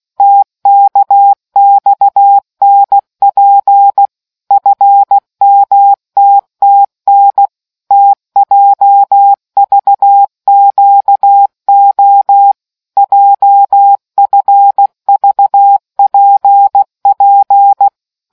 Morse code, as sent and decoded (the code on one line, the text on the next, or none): - -.- -..- -. .--. / ..-. -- - - -. / - .--- ...- --.- --- / .--- ..-. ...- .--. .--.
TKXNP FMTTN TJVQO JFVPP